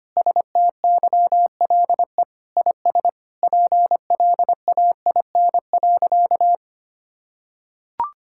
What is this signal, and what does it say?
Morse code, heard as a keyed tone, about 25 wpm, 700 Hz. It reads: STYLE IS PLAIN.
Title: Sentences from Top 1000 Words - Review Part 5 - 25wpm